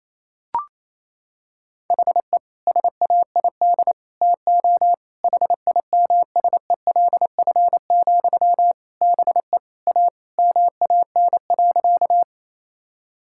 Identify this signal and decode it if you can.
Morse code, heard as a keyed tone, about 28 wpm, 700 Hz. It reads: HE SAID TO HIMSELF, BE A MAN.
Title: Sentences from Top 1600 Words 28wpm